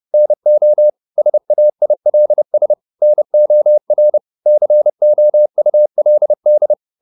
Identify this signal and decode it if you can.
Morse code, heard as a keyed tone, about 30 wpm, 600 Hz. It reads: NO SAILS NOR COULD